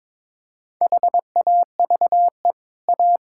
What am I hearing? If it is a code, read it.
Morse code, heard as a keyed tone, about 22 wpm, 690 Hz. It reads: HAVE A
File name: practice / 22wpm / Easy Sentences with Hints - 22wpm